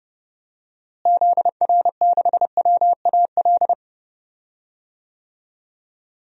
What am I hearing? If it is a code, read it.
Morse code, heard as a keyed tone, about 30 wpm, 700 Hz. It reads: ZR6WAL
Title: DXCC-100 Call Signs - 6 Characters - Repeated 3x - 30wpm